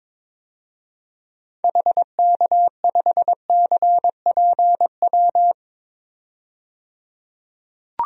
Morse code, heard as a keyed tone, about 22 wpm, 700 Hz. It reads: HK5CPW
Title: DXCC-100 Call Signs - 6 Characters - 22wpm